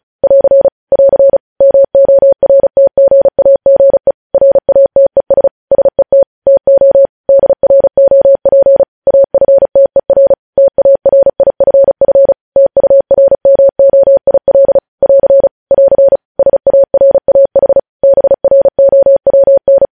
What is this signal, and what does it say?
Morse code, heard as a keyed tone, about 36 wpm, 550 Hz. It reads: <AR> <AR> MORTGAGE RATES SET TO DROP AFTER TARIFF TURMOIL <AR> <AR> SARAH BROWN
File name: podcast / News headlines-1x-35x35